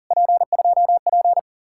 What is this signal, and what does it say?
Morse code, heard as a keyed tone, about 52 wpm, 700 Hz. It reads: P2P